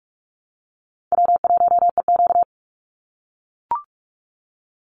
Morse code, heard as a keyed tone, about 44 wpm, 700 Hz. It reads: W1EQ
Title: Call Signs - 45wpm